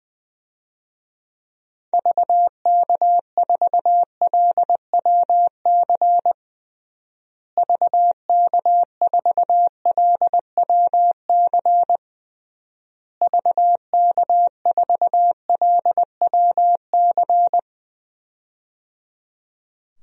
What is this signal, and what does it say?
Morse code, heard as a keyed tone, about 20 wpm, 700 Hz. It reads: VK4LWC VK4LWC VK4LWC
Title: DXCC-100 Call Signs - 6 Characters - Repeated 3x - 20wpm